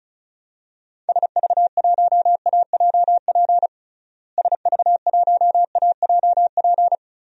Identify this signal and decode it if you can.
Morse code, heard as a keyed tone, about 34 wpm, 700 Hz. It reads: SV1AJP SV1AJP
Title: DXCC-100 Call Signs - 6 Characters - Repeated 3x - 35wpm